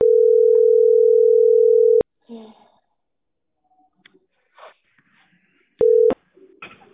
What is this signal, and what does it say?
Ringback tone (North America), 440 + 480 Hz.